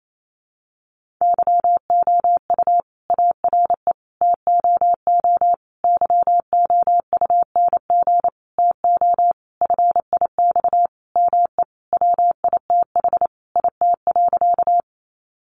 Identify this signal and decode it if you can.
Morse code, heard as a keyed tone, about 28 wpm, 700 Hz. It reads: YOU ARE TOO YOUNG TO FIX ME WITH IT.
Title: Sentences from Top 1400 Words 28wpm